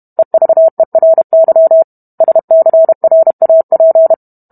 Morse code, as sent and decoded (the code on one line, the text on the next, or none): . ...- . .-. -.-- / ... -.-. .-. .- .--.
EVERY SCRAP